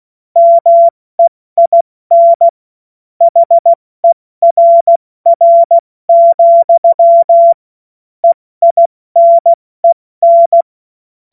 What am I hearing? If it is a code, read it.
Morse code, heard as a keyed tone, about 16 wpm, spaced as for 14 wpm, 680 Hz. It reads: MEIN HERR, EINEN
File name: Morsekurs / Witze / Witz 15. November 2022 70 F80